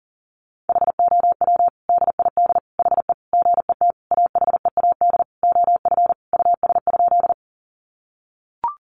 Morse code, heard as a keyed tone, about 40 wpm, 700 Hz. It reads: HOW DID HE GET AHEAD OF US?
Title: Sentences from Top 1100 Words 40wpm